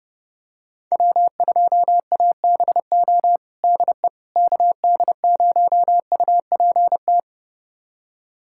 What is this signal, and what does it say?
Morse code, heard as a keyed tone, about 30 wpm, 700 Hz. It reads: W2ABO DE KD0UPT
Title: USA Callsign DE Callsign 30wpm